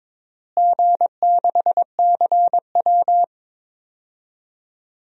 Morse code, heard as a keyed tone, about 22 wpm, 700 Hz. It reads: G6CW
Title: DXCC-100 Call Signs - 4 Characters - Repeated 3x - 22wpm